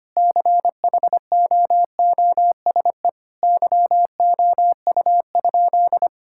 Morse code, heard as a keyed tone, about 25 wpm, 700 Hz. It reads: CHOOSE YOU?
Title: Sentences from Top 1000 Words - Review Part 6 - 25wpm